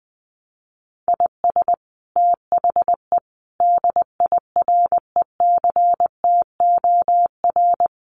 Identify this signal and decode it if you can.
Morse code, heard as a keyed tone, about 20 wpm, 700 Hz. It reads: IS THE DIRECTOR
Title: Sets of 5 Words Encore2 20wpm